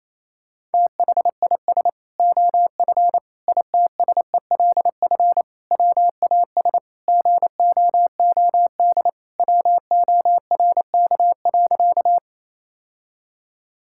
Morse code, heard as a keyed tone, about 28 wpm, 700 Hz. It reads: THIS OF ITSELF WAS GOOD WORK.
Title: Sentences from Top 1000 Words - Review Part 5 - 28wpm